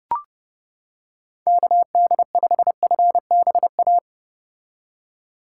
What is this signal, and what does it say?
Morse code, heard as a keyed tone, about 30 wpm, 700 Hz. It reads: KD5FBA
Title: K1USN SST QSO Practice - 30wpm